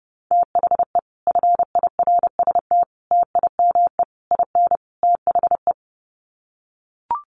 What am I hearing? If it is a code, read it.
Morse code, heard as a keyed tone, about 28 wpm, 700 Hz. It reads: THE FIRST TIME IN THE
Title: Sets of 5 Words Encore 30wpm